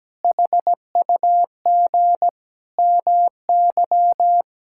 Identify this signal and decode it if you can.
Morse code, heard as a keyed tone, about 17 wpm, 700 Hz. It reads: HUG MY
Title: Pangram Challenge v2 - 17wpm